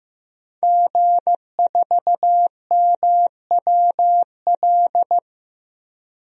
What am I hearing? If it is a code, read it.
Morse code, heard as a keyed tone, about 15 wpm, 700 Hz. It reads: G4MWL
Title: DXCC-100 Call Signs - 5 Characters - Repeated 2x - 15wpm